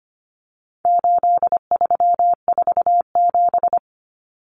Morse code, heard as a keyed tone, about 25 wpm, 700 Hz. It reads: 8347
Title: Numbers - 4-Digit 25wpm